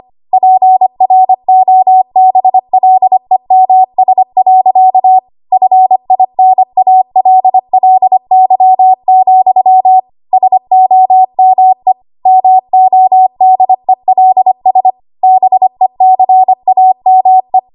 Morse code, heard as a keyed tone, about 25 wpm, 750 Hz. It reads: PROBLEMS. FINALLY, SOME MODELS BECAME
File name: Archive / 25 WPM / 25WPM